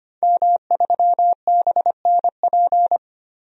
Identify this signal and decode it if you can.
Morse code, heard as a keyed tone, about 25 wpm, 700 Hz. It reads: M3BNP
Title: DXCC-100 Call Signs - 5 Characters - 25wpm